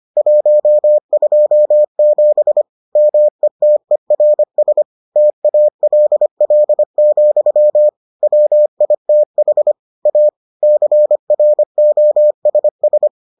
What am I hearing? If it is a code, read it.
Morse code, heard as a keyed tone, about 25 wpm, 600 Hz. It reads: 127 METERS TALL, WITH A CROSS